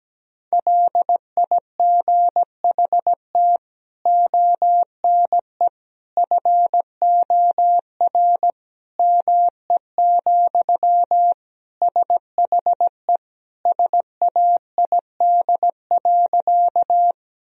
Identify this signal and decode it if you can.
Morse code, heard as a keyed tone, about 17 wpm, 700 Hz. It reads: LIGHT ONE FOR ME, SHE SAID.
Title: Sentences from Top 1000 Words - Review Part 1 - 17wpm